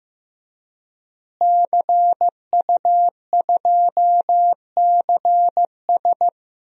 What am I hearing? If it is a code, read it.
Morse code, heard as a keyed tone, about 15 wpm, 700 Hz. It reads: CU2CS
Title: DXCC-100 Call Signs - 5 Characters - Repeated 3x - 15wpm